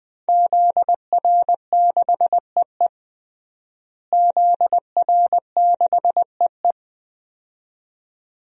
Morse code, heard as a keyed tone, about 20 wpm, 700 Hz. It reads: ZR6EE ZR6EE
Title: DXCC-100 Call Signs - 5 Characters - Repeated 2x - 20wpm